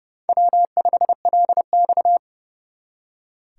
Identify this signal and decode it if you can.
Morse code, heard as a keyed tone, about 38 wpm, 700 Hz. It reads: W5LX